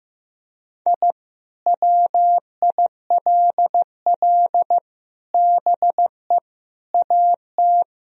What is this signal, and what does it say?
Morse code, heard as a keyed tone, about 15 wpm, 700 Hz. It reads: I WILL BE AT
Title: Sentences from Top 1000 Words - Review Part 3 - 15wpm